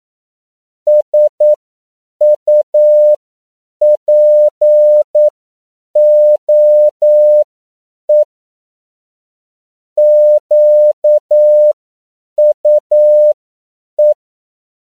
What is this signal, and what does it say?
Morse code, heard as a keyed tone, about 9 wpm, spaced as for 7 wpm, 600 Hz. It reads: SUPOE QUE